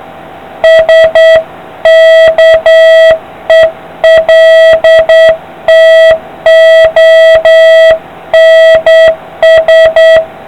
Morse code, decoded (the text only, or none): SKELTONS